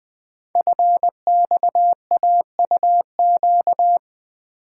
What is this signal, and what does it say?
Morse code, heard as a keyed tone, about 20 wpm, 700 Hz. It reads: FXAUQ
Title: Code Groups - 5 Letters - 20wpm